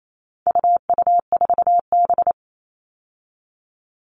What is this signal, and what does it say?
Morse code, heard as a keyed tone, about 28 wpm, 700 Hz. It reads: UU4B
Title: All Cut Numbers - 4-Digit 28wpm